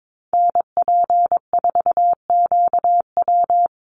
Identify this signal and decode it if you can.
Morse code, heard as a keyed tone, about 22 wpm, 700 Hz. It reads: NP4QW
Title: USA Call Signs Repeated 3x 22wpm